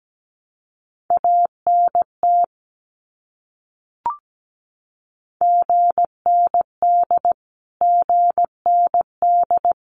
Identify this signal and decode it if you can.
Morse code, heard as a keyed tone, about 17 wpm, 700 Hz. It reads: ANT GND GND
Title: Abbreviations Repeated 3x - 17wpm